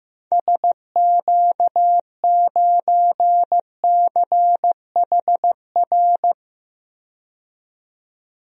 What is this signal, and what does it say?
Morse code, heard as a keyed tone, about 15 wpm, 700 Hz. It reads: SQ9CHR
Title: DXCC-100 Call Signs - 6 Characters - 15wpm